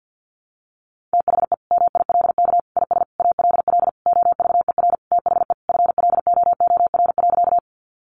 Morse code, heard as a keyed tone, about 50 wpm, 700 Hz. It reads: THE MILK IS ALL OVER THE FLOOR.
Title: Additional Easy Phrases 50wpm